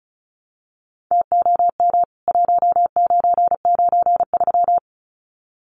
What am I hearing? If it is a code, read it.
Morse code, heard as a keyed tone, about 34 wpm, 700 Hz. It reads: TOM 1993
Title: CWT Exchange 35wpm